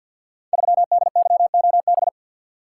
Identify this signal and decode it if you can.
Morse code, heard as a keyed tone, about 47 wpm, 700 Hz. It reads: 3DYOB
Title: Code Groups - 5 Letter:Numbers - 50wpm